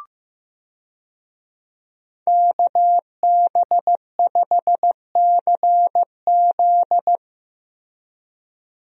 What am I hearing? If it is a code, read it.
Morse code, heard as a keyed tone, about 16 wpm, 700 Hz. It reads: KB5CZ
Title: USA Call Signs - 5 Characters - Repeated 2x - 15wpm